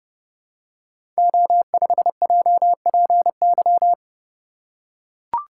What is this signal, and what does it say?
Morse code, heard as a keyed tone, about 30 wpm, 700 Hz. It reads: O5JPY
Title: Code Groups - 5 Letter:Numbers - 30wpm